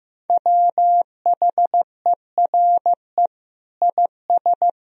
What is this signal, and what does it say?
Morse code, heard as a keyed tone, about 15 wpm, 700 Hz. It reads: WHERE IS
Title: Sentences from Top 1000 Words - Review Part 4 - 15wpm